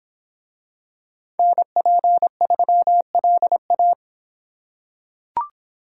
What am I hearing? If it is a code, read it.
Morse code, heard as a keyed tone, about 25 wpm, 700 Hz. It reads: NP3LA